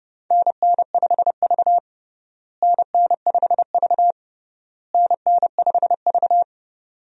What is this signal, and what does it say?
Morse code, heard as a keyed tone, about 32 wpm, 700 Hz. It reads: NN5V NN5V NN5V
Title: USA Call Signs - 4 Characters - Repeated 3x - 30wpm